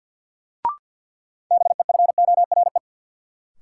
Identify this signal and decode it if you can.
Morse code, heard as a keyed tone, about 47 wpm, 700 Hz. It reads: BEFORE
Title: Sets of 2 Words - New Word Drill - 50wpm